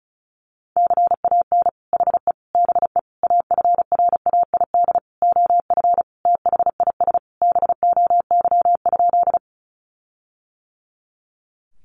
Morse code, heard as a keyed tone, about 34 wpm, 700 Hz. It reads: CAN HE BE AFRAID OF THIS BOY?
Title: Sentences from Top 1500 Words 35wpm